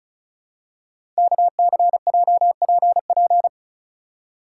Morse code, heard as a keyed tone, about 36 wpm, 690 Hz. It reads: KCJPP
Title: Code Groups - 5 Letters - 35wpm